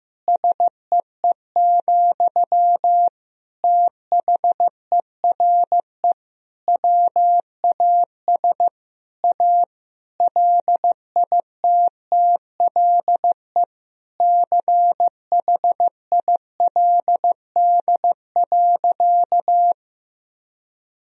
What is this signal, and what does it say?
Morse code, heard as a keyed tone, about 15 wpm, 700 Hz. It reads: SEE, THERE WAS A LITTLE CHILD.
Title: Sentences from Top 1000 Words - Review Part 1 - 15wpm